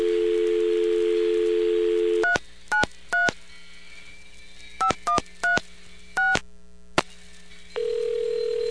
DTMF, heard as telephone touch tones, tones of about 120 ms. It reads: 393 5136